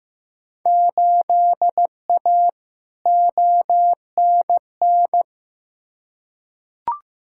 Morse code, heard as a keyed tone, about 15 wpm, 700 Hz. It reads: 8A ONN
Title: Field Day Exchanges Encore - 15wpm